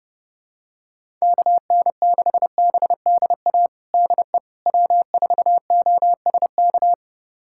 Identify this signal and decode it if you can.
Morse code, heard as a keyed tone, about 31 wpm, 700 Hz. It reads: KN6BDA DE W4OSK
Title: USA Callsign DE Callsign 30wpm